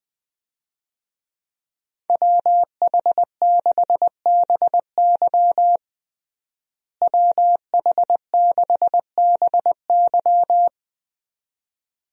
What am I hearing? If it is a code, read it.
Morse code, heard as a keyed tone, about 20 wpm, 700 Hz. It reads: WH6BY WH6BY
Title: USA Call Signs - 5 Characters - Repeated 2x - 20wpm